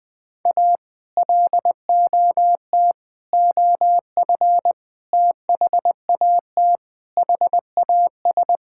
Morse code, heard as a keyed tone, about 19 wpm, 700 Hz. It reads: A LOT OF THAT HAS
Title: Sentences with 8-Word Phrases - Spoken Beginning - 20wpm